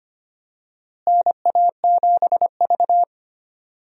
Morse code, heard as a keyed tone, about 25 wpm, 700 Hz. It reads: NA7V
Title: USA Call Signs - 4 Characters - Repeated 2x - 25wpm